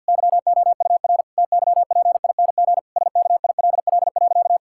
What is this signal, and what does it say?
Morse code, heard as a keyed tone, about 51 wpm, 700 Hz. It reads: YOUR TYPING SKILL.
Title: Pangram Challenge v2 - 50wpm